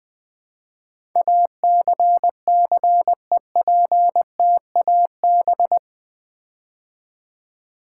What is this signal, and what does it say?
Morse code, heard as a keyed tone, about 20 wpm, 700 Hz. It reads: ACCEPTAB
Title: Build A Word - 4 Letters At A Time - 10-Letter Words - 20wpm